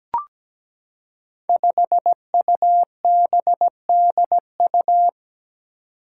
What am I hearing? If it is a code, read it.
Morse code, heard as a keyed tone, about 17 wpm, 700 Hz. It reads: 5UBDU
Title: Five Challenging Letter-Number Combinations 17wpm